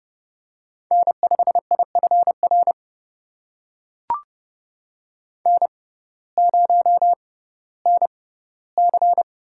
Morse code, heard as a keyed tone, about 30 wpm, spaced as for 17 wpm, 700 Hz. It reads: N5IFR N0NC
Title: USA Call Signs - 6x Character Spacing - 30wpm